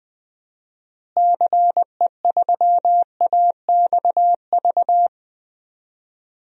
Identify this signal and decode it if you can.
Morse code, heard as a keyed tone, about 20 wpm, 700 Hz. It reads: CE3AXV
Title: DXCC-100 Call Signs - 6 Characters - Repeated 2x - 20wpm